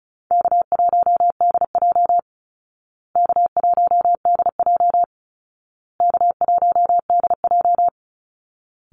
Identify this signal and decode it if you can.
Morse code, heard as a keyed tone, about 34 wpm, 700 Hz. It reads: K1DJ K1DJ K1DJ
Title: Easy Call Signs Repeated 3x - 35wpm